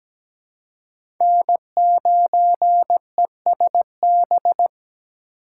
Morse code, heard as a keyed tone, about 17 wpm, 700 Hz. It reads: N9ESB
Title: USA Call Signs - 5 Characters - 3x Character Spacing - 17wpm